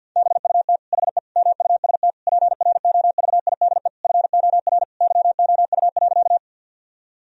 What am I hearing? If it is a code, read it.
Morse code, heard as a keyed tone, about 50 wpm, 700 Hz. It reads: BUT HE MUST PROVIDE FOR YOU.
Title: Sentences from Top 1000 Words - Review Part 3 - 50wpm